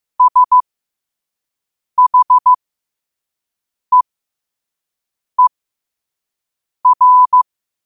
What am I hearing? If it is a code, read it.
Morse code, heard as a keyed tone, about 15 wpm, spaced as for 5 wpm, 1000 Hz. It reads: SHEER
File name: qotdfiles / 5wpm